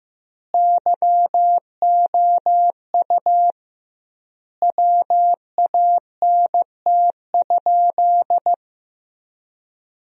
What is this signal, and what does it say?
Morse code, heard as a keyed tone, about 15 wpm, 700 Hz. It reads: YOU WANT?
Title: 2x Word-Spacing - Sentences from Top 200 Words - 15wpm